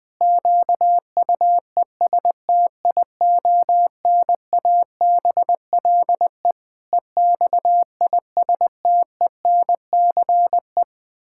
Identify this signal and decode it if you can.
Morse code, heard as a keyed tone, about 20 wpm, 700 Hz. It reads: QUESTIONABLE EXISTENCE
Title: Pangram Challenge v2 - 20wpm